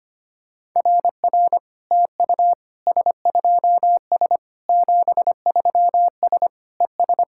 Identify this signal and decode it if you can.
Morse code, heard as a keyed tone, about 25 wpm, 700 Hz. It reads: RR TU S2S 73S ES
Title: SOTA QSO Practice 25wpm